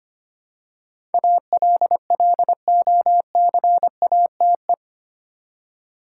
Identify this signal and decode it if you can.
Morse code, heard as a keyed tone, about 25 wpm, 700 Hz. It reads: ALLOCATE
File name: 8 Letter Words - First 3 Letters Given - 25wpm